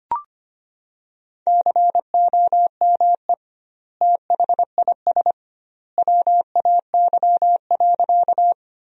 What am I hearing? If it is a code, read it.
Morse code, heard as a keyed tone, about 25 wpm, 700 Hz. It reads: COME THIS WAY.
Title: 2x Word-Spacing - Sentences from Top 100 Words - 25wpm